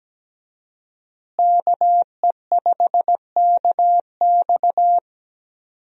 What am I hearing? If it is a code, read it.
Morse code, heard as a keyed tone, about 17 wpm, 700 Hz. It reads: KE5KX
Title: USA Call Signs - 5 Characters - Repeated 3x - 17wpm